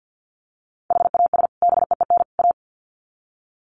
Morse code, heard as a keyed tone, about 50 wpm, 700 Hz. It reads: HAS BEEN A
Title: Sets of 3 Words 50wpm